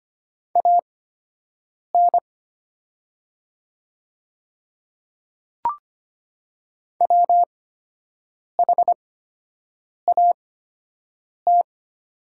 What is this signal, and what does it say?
Morse code, heard as a keyed tone, about 25 wpm, spaced as for 7 wpm, 700 Hz. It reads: AN WHAT
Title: 8x Character Spacing - Top 200 Words - 25wpm